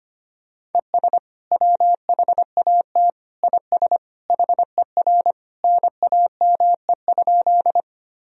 Morse code, heard as a keyed tone, about 25 wpm, 700 Hz. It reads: ES WHAT IS HER NAME?
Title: Sentences from Top 200 Words and Abbreviations 25wpm